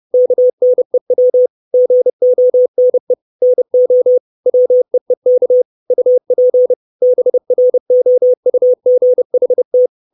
Morse code, heard as a keyed tone, about 30 wpm, 500 Hz. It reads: KNEW GONE NO WEEK UP BROUGHT
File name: 632 Words 30WPM